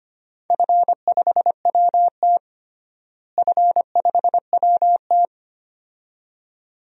Morse code, heard as a keyed tone, about 25 wpm, 700 Hz. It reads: F5WT F5WT
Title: DXCC-100 Call Signs - 4 Characters - Repeated 2x - 25wpm